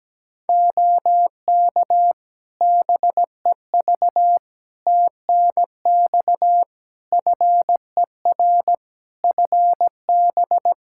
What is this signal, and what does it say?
Morse code, heard as a keyed tone, about 17 wpm, 700 Hz. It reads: OK BEV TNX FER FB
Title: Practice QSO - Exchange 4 - Ending - 17wpm